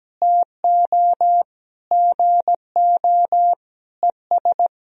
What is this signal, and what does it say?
Morse code, heard as a keyed tone, about 17 wpm, 700 Hz. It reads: TO GO ES
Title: Sentences from Top 200 Words and Abbreviations 17wpm